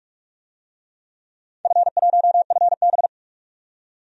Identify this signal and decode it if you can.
Morse code, heard as a keyed tone, about 47 wpm, 700 Hz. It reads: F1FB
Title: DXCC-100 Call Signs - 4 Characters - 45wpm